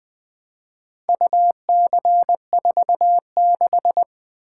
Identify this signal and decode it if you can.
Morse code, heard as a keyed tone, about 20 wpm, 700 Hz. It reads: UC46
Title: Four Letter-Number Combinations - 20wpm